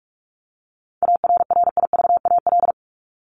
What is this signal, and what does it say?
Morse code, heard as a keyed tone, about 44 wpm, 700 Hz. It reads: ARRIVAL
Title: Top 1751-2000 Words 45wpm